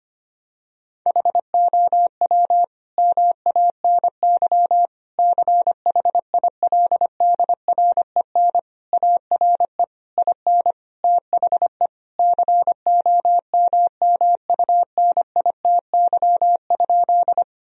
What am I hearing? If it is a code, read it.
Morse code, heard as a keyed tone, about 25 wpm, 700 Hz. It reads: HOW MANY CHILDREN ARE IN THE COMMUNITY?
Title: Sentences from Top 1000 Words - Review Part 4 - 25wpm